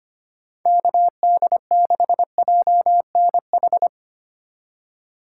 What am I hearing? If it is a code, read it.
Morse code, heard as a keyed tone, about 25 wpm, 700 Hz. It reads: KD6JNH
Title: USA Call Signs - 6 Characters - 4x Character Spacing - 25wpm